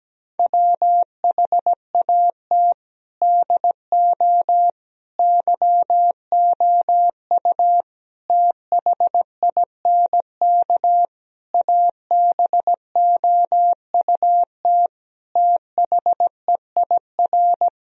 Morse code, heard as a keyed tone, about 17 wpm, 700 Hz. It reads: WHAT DO YOU THINK ABOUT THEIR
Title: Sentences from Top 1000 Words - Review Part 4 - 17wpm